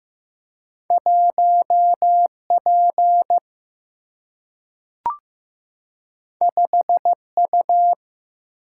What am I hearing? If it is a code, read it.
Morse code, heard as a keyed tone, about 15 wpm, 700 Hz. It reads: 1P 5U
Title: Single-Letter Single-Number 15wpm